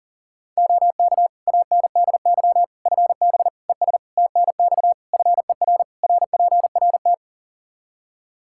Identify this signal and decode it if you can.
Morse code, heard as a keyed tone, about 40 wpm, 700 Hz. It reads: OK ANDY FB ES TNX FER RPRT
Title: K1USN SST QSO Practice 40wpm